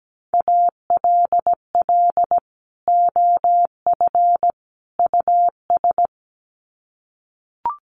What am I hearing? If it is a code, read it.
Morse code, heard as a keyed tone, about 17 wpm, 700 Hz. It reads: ALL OF US
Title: Sets of 3 Words 17wpm